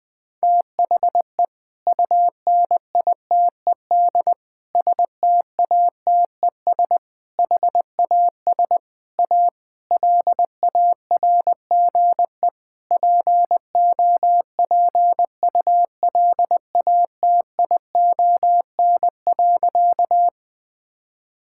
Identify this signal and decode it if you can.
Morse code, heard as a keyed tone, about 20 wpm, 700 Hz. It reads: THE UNITED STATES HAS A LARGE POPULATION.
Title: Sentences with 4-Word Phrases - 20wpm